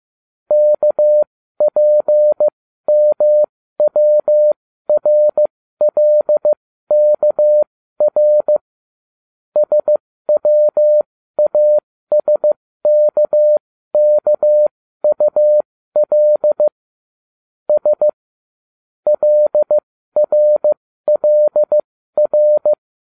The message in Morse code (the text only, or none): KPMWRLKR SWASKKUL S LRLR